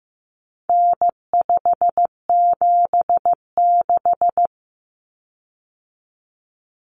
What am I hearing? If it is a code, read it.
Morse code, heard as a keyed tone, about 15 wpm, 700 Hz. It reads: N576
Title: Common Cut Numbers - 4-Digit 15wpm